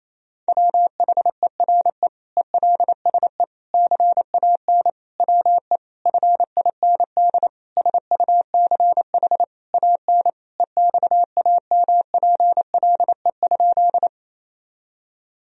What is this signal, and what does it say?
Morse code, heard as a keyed tone, about 28 wpm, 700 Hz. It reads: WHERE ELSE CAN WE FIND SUCH AN EXAMPLE?
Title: Sentences from Top 1000 Words - Review Part 1 - 28wpm